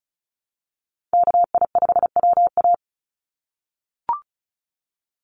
Morse code, heard as a keyed tone, about 34 wpm, 700 Hz. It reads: KI5WA
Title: NA SOTA Chaser Callsigns 35wpm